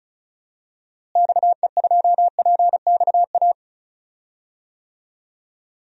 Morse code, heard as a keyed tone, about 34 wpm, 700 Hz. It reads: XE2PXA
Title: DXCC-100 Call Signs - 6 Characters - Repeated 3x - 35wpm